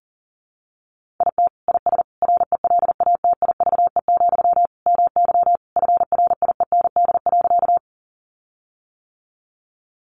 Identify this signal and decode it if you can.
Morse code, heard as a keyed tone, about 40 wpm, 700 Hz. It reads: IT IS RELATIVE, MY FRIEND.
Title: Sentences from Top 1400 Words 40wpm